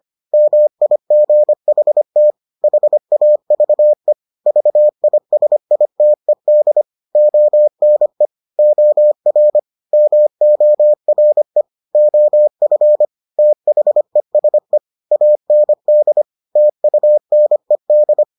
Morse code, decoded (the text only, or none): MIGHT HAVE VISITED ONE OR MORE OF THESE AND TUNED